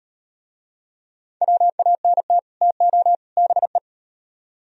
Morse code, heard as a keyed tone, about 35 wpm, 700 Hz. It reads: WANT TO BE